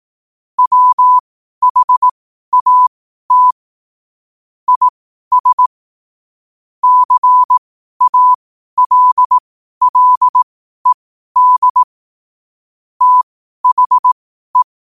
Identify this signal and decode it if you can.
Morse code, heard as a keyed tone, about 18 wpm, spaced as for 12 wpm, 1000 Hz. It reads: WHAT IS CALLED THE